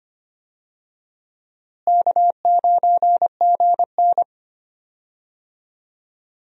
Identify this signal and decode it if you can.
Morse code, heard as a keyed tone, about 25 wpm, 700 Hz. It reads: K9GN
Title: USA Call Signs - 4 Characters - Repeated 3x - 25wpm